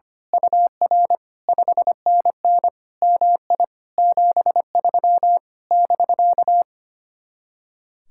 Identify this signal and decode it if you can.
Morse code, heard as a keyed tone, about 24 wpm, 700 Hz. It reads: UR 5NN MI 73 <BK>
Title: Speed-Racing Common QSO Phrases - 25wpm